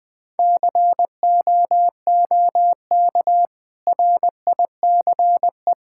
Morse code, heard as a keyed tone, about 20 wpm, 700 Hz. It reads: COOK RICE
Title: Easy Sentences with Hints - 20wpm